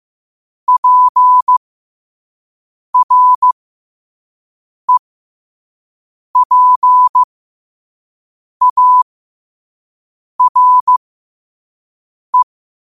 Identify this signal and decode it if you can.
Morse code, heard as a keyed tone, about 15 wpm, spaced as for 5 wpm, 1000 Hz. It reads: PREPARE
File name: qotdfiles / 5wpm